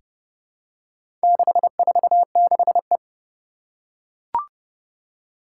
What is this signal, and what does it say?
Morse code, heard as a keyed tone, about 32 wpm, 700 Hz. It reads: 646E
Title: Four Challenging Letter-Number Combinations 30wpm